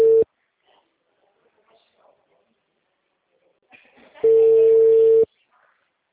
Ringback tone (China, 450 Hz).
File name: out- -2014- - - .69454